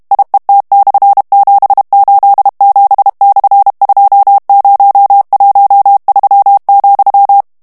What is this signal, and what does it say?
Morse code, heard as a keyed tone, about 32 wpm, 800 Hz. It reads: IET/787/2013,
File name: morse mp3 / ESPAGNOL TEXTE EN CLAIR 2 VITESSE 30 MOTS MINUTES